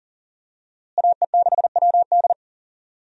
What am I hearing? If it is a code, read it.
Morse code, heard as a keyed tone, about 39 wpm, 700 Hz. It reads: AE6WD
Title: USA Call Signs - 5 Characters - Repeated 3x - 40wpm